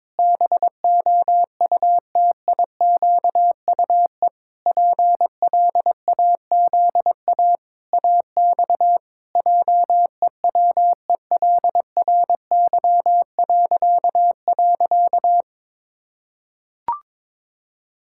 Morse code, heard as a keyed tone, about 21 wpm, 700 Hz. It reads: BOUTIQUE PLAZA AX JEWELRY..
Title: Pangram Challenge v2 - 22wpm